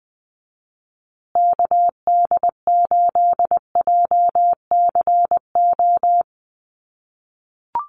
Morse code, heard as a keyed tone, about 20 wpm, 700 Hz. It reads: KD8JCO
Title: USA Call Signs Repeated 3x 20wpm